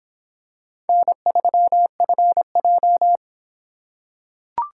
Morse code, heard as a keyed tone, about 25 wpm, 700 Hz. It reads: N3FJ